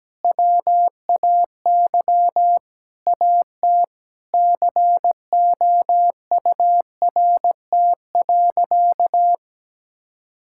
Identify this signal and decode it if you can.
Morse code, heard as a keyed tone, about 17 wpm, 700 Hz. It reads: WAY AT COURT.
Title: Sentences from Top 1600 Words 17wpm